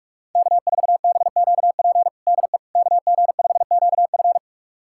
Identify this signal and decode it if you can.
Morse code, heard as a keyed tone, about 46 wpm, 700 Hz. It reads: K4BQP DE KG5QF
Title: USA Callsign DE Callsign 45wpm